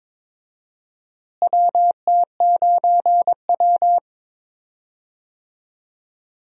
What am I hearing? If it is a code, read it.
Morse code, heard as a keyed tone, about 22 wpm, 700 Hz. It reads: WT9W
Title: USA Call Signs - 4 Characters - Repeated 3x - 22wpm